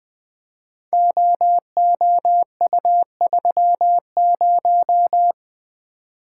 Morse code, heard as a keyed tone, about 20 wpm, 700 Hz. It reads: OOU30
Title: Code Groups - 5 Letter:Numbers - 20wpm